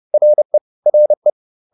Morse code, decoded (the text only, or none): RE RE